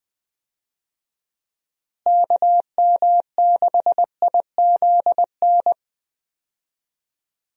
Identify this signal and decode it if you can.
Morse code, heard as a keyed tone, about 20 wpm, 700 Hz. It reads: KM6IZN
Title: USA Call Signs - 6 Characters - Repeated 3x - 20wpm